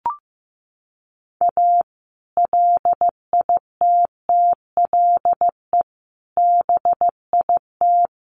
Morse code, heard as a keyed tone, about 15 wpm, 700 Hz. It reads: A LITTLE BIT
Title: Sets of 5 Words Encore2 15wpm